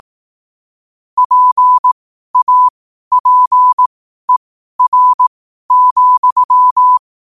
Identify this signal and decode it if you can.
Morse code, heard as a keyed tone, about 18 wpm, spaced as for 12 wpm, 1000 Hz. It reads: PAPER,
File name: qotdfiles / 12wpm